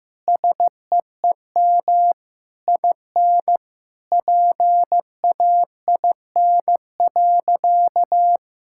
Morse code, heard as a keyed tone, about 15 wpm, 700 Hz. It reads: SEEM IN PAIN.
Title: Sentences from Top 1600 Words 15wpm